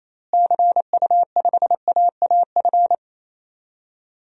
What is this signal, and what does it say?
Morse code, heard as a keyed tone, about 28 wpm, 700 Hz. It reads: CU5AAF
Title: DXCC-100 Call Signs - 6 Characters - Repeated 3x - 28wpm